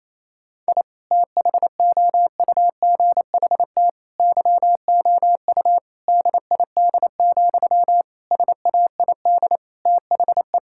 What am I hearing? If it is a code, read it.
Morse code, heard as a keyed tone, about 28 wpm, 700 Hz. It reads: I THOUGHT YOU DID, SAID THE
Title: Sentences from Top 1000 Words - Review Part 5 - 28wpm